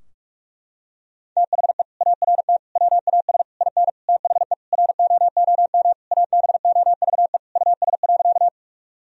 Morse code, heard as a keyed tone, about 46 wpm, 700 Hz. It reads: THE ART WAS IN THE ROOM ABOVE US.
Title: Sentences from Top 1000 Words - Review Part 3 - 45wpm